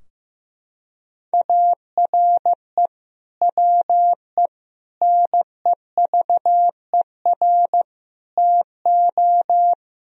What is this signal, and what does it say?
Morse code, heard as a keyed tone, about 16 wpm, 700 Hz. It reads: ARE WE NEVER TO
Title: Sentences from Top 1000 Words - Review Part 3 - 15wpm